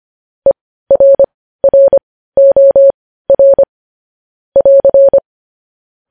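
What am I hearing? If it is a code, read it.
Morse code, heard as a keyed tone, about 25 wpm, spaced as for 15 wpm, 550 Hz. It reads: ERROR <AR>
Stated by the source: Morse news headlines sent at 15 WPM, beginning with vvv.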